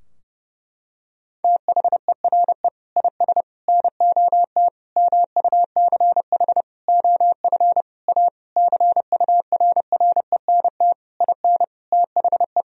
Morse code, heard as a keyed tone, about 30 wpm, 700 Hz. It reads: THERE IS NOT MUCH OF A CURRENT IN THE
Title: Sentences from Top 1000 Words - Review Part 4 - 30wpm